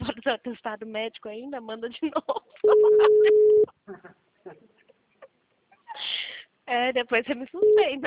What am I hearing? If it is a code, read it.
Ringback tone (Europe, 425 Hz), with speech.